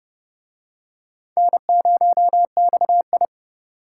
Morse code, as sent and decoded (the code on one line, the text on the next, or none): -. ----- -..- ..
N0XI